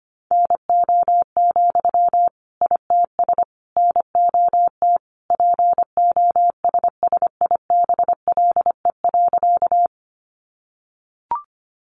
Morse code, heard as a keyed tone, about 25 wpm, 700 Hz. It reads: NO, ITS NOT POSSIBLE.
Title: Sentences from Top 600 Words 25wpm